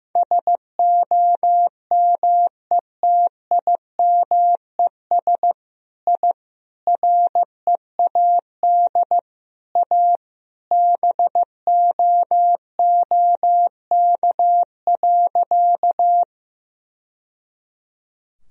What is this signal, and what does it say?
Morse code, heard as a keyed tone, about 15 wpm, 700 Hz. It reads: SOMETIMES I READ A BOOK.
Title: Sentences from Top 1000 Words - Review Part 3 - 15wpm